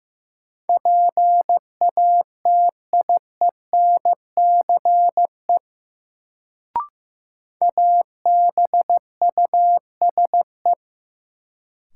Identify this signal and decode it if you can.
Morse code, heard as a keyed tone, about 16 wpm, 700 Hz. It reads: PATIENCE ABUSE
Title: Sentences from Top 1300 Words - New Word Drill Alt - 15wpm